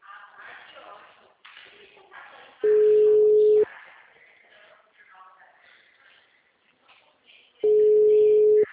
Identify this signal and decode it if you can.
Ringback tone (Europe, 425 Hz), with speech.
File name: out- -2041- - - .817